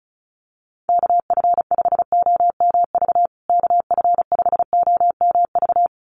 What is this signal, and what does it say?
Morse code, heard as a keyed tone, about 34 wpm, 700 Hz. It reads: KF5OMV KF5OMV
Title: USA Call Signs Repeated 2x 35wpm